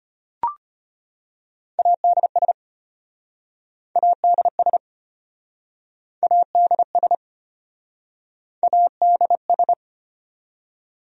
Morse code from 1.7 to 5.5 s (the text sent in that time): ADS ADS